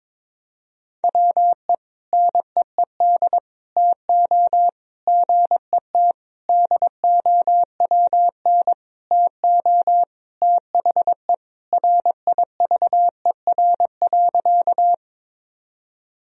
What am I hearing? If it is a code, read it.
Morse code, heard as a keyed tone, about 22 wpm, 700 Hz. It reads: WE NEED TO GET DOWN TO THE RIVER.
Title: Sentences from Top 1000 Words - Review Part 1 - 22wpm